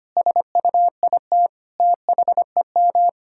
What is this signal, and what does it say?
Morse code, heard as a keyed tone, about 25 wpm, 700 Hz. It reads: SUIT THEM
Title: Sentences from Top 1600 Words 25wpm